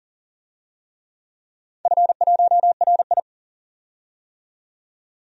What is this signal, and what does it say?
Morse code, heard as a keyed tone, about 42 wpm, 700 Hz. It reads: F1RI
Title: DXCC-100 Call Signs - 4 Characters - 40wpm